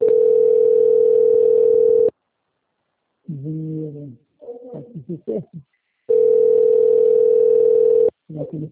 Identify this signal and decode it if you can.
Ringback tone (North America), 440 + 480 Hz, with speech.
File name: out- - - - - .10069